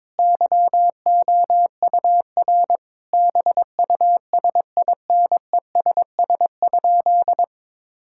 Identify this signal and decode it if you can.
Morse code, heard as a keyed tone, about 22 wpm, 700 Hz. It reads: YOUR BUSINESS?
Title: Sentences from Top 1000 Words - Review Part 5 - 22wpm